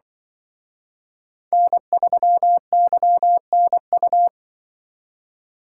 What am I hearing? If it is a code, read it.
Morse code, heard as a keyed tone, about 24 wpm, 700 Hz. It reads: N3YNU
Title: Speed Racing - USA 1x3 Call Signs 20wpm